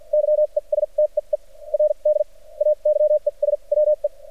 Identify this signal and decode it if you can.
Morse code, heard as a keyed tone, about 47 wpm, spaced as for 32 wpm, 600 Hz. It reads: YESTEERDAY ES WE
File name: archived / CRSnet